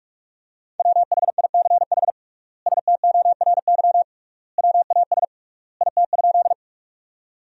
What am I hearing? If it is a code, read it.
Morse code, heard as a keyed tone, about 46 wpm, 700 Hz. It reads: WHICH STORY WAS IT?
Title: 3x Word-Spacing - Sentences from Top 300 Words - 45wpm